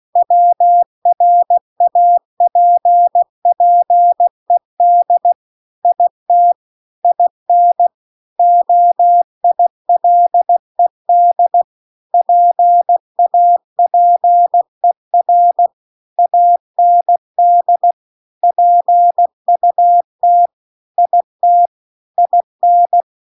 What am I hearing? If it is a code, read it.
Morse code, heard as a keyed tone, about 16 wpm, 700 Hz. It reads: WRAPPED IT IN OILED PAPER AND PUT IT IN
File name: CW-The-Big-Twoheaded-River-2-16WPM